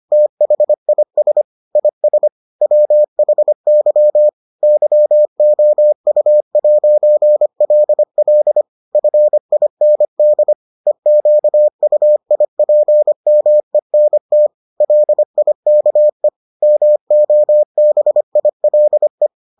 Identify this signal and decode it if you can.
Morse code, heard as a keyed tone, about 25 wpm, 600 Hz. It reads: THIS IS WHY YOU'LL FIND EQUIPMENT LIKE MOBILE